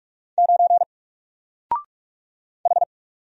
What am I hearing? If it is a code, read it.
Morse code, heard as a keyed tone, about 44 wpm, 700 Hz. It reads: 9 H
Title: Single Character (Letter:Number:Punctuation:Prosign) Alt 45wpm